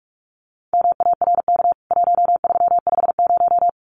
Morse code, heard as a keyed tone, about 44 wpm, 700 Hz. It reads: MARK 1350
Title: CWT Exchange 45wpm